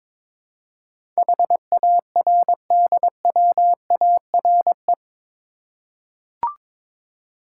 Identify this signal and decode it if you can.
Morse code, heard as a keyed tone, about 22 wpm, 700 Hz. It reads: HARDWARE
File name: practice / 22wpm / Eight Letter Words - 22wpm